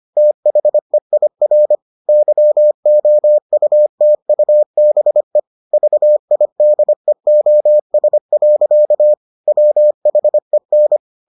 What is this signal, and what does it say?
Morse code, heard as a keyed tone, about 25 wpm, 600 Hz. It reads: THEIR YOUTUBE VIDEOS. WHEN